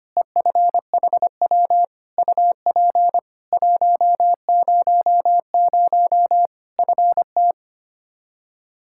Morse code, heard as a keyed tone, about 25 wpm, 700 Hz. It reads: EFHW UP 100 FT
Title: Practice QSO - Exchange 2 - Info - 25wpm